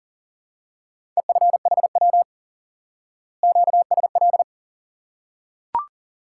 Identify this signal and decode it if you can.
Morse code, heard as a keyed tone, about 40 wpm, 700 Hz. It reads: EFHW QSL
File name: IQE Course 3 - Lesson 89 - YAGI - Two Elements - 40wpm